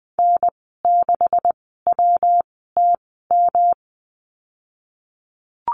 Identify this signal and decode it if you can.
Morse code, heard as a keyed tone, about 20 wpm, spaced as for 15 wpm, 700 Hz. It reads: N6WTM
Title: 161 Lesson 26g - Letter g - Call Signs 20wpm